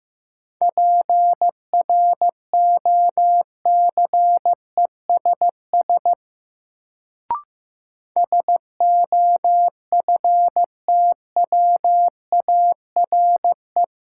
Morse code, heard as a keyed tone, about 15 wpm, 700 Hz. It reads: PROCESS SOFTWARE
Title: Sentences from Top 1000 Words - Review Part 3 - New Word Drill - 15wpm